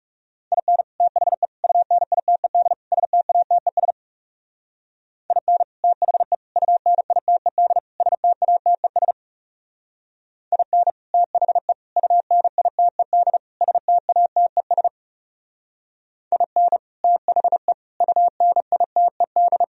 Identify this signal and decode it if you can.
Morse code, heard as a keyed tone, about 36 wpm, 700 Hz. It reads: IN THE UNITED STATES IN THE UNITED STATES IN THE UNITED STATES IN THE UNITED
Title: Speed-Racing - Sets of 4 Words 30wpm